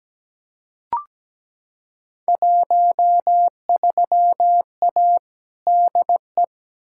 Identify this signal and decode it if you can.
Morse code, heard as a keyed tone, about 17 wpm, 700 Hz. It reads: 13A DE
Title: Field Day Exchanges Encore - 17wpm